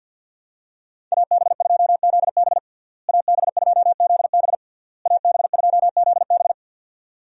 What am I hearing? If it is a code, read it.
Morse code, heard as a keyed tone, about 52 wpm, 700 Hz. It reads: AB2ZB AB2ZB AB2ZB
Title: USA Call Signs - 5 Characters - Repeated 3x - 50wpm